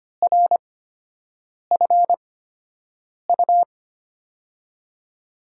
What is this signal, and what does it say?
Morse code, heard as a keyed tone, about 25 wpm, spaced as for 7 wpm, 700 Hz. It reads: RFU
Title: USA Call Signs - 5 Characters - 8x Character Spacing - 25wpm